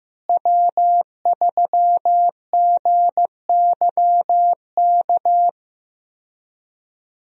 Morse code, heard as a keyed tone, about 15 wpm, 700 Hz. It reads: W3GYK
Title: USA Call Signs - 5 Characters - Repeated 3x - 15wpm